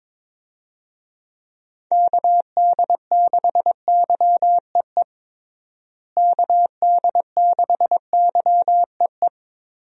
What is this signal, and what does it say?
Morse code, heard as a keyed tone, about 22 wpm, 700 Hz. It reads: KD6YEE KD6YEE
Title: USA Call Signs - 6 Characters - Repeated 3x - 22wpm